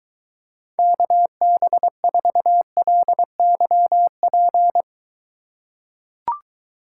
Morse code, heard as a keyed tone, about 23 wpm, 700 Hz. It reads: KB4LYP